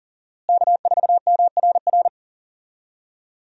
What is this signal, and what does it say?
Morse code, heard as a keyed tone, about 40 wpm, 700 Hz. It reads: K4MRR
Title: USA Call Signs - 5 Characters - Repeated 3x - 40wpm